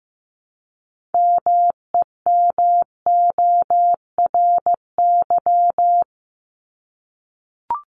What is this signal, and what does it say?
Morse code, heard as a keyed tone, about 15 wpm, 700 Hz. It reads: MEMORY
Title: Top 1251-1500 Words 15wpm